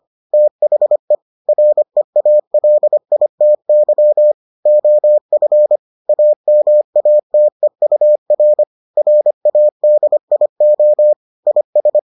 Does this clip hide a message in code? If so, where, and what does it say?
Yes: Morse code from 0.3 to 12.0 s, heard as a keyed tone, about 25 wpm, 600 Hz. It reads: THE REALITY OF AMATEUR RADIO IS